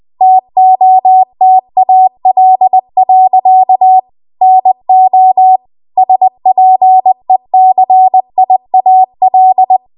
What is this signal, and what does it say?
Morse code, heard as a keyed tone, about 20 wpm, 750 Hz. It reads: TOTAL. NO SPECIAL